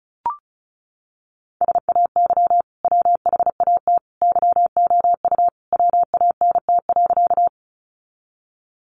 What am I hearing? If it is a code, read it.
Morse code, heard as a keyed tone, about 34 wpm, 700 Hz. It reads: SAY WHAT YOU WANT.
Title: Sentences from Top 200 Words 35wpm